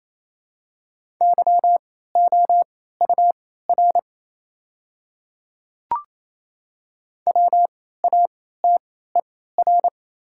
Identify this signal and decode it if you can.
Morse code, heard as a keyed tone, about 28 wpm, spaced as for 16 wpm, 700 Hz. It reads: YOUR WATER
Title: 3x Character Spacing - Top 200 Words - 28wpm